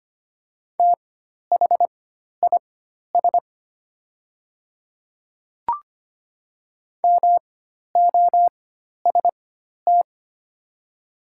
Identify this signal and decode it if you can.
Morse code, heard as a keyed tone, about 25 wpm, spaced as for 12 wpm, 700 Hz. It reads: THIS MOST
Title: 4x Character Spacing - Top 100 Words - 25wpm